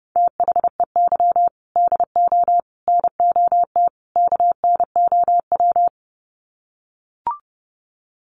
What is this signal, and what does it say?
Morse code, heard as a keyed tone, about 30 wpm, 700 Hz. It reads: THEY DO NOT KNOW
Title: Sets of 4 Words 30wpm